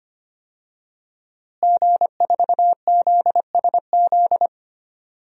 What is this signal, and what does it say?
Morse code, heard as a keyed tone, about 25 wpm, 700 Hz. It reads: G4ZSZ
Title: DXCC-100 Call Signs - 5 Characters - Repeated 3x - 25wpm